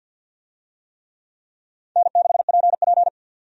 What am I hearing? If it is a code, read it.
Morse code, heard as a keyed tone, about 52 wpm, 700 Hz. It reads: N6PP
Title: USA Call Signs - 4 Characters - 50wpm